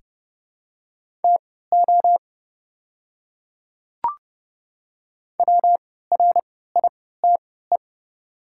Morse code, heard as a keyed tone, about 30 wpm, spaced as for 17 wpm, 700 Hz. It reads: TO WRITE
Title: 3x Character Spacing - Top 100 Words - 30wpm